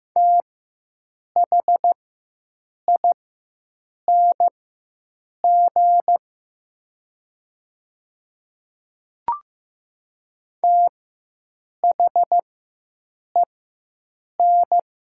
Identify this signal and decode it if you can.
Morse code, heard as a keyed tone, about 15 wpm, spaced as for 7 wpm, 700 Hz. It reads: THING THEN
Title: 4x Character Spacing - Top 100 Words - 15wpm